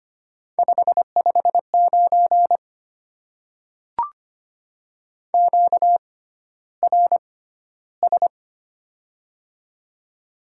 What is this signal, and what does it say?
Morse code, heard as a keyed tone, about 25 wpm, spaced as for 13 wpm, 700 Hz. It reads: 559 QRS
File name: Top 90 QSO Elements - 6x Character Spacing - 25wpm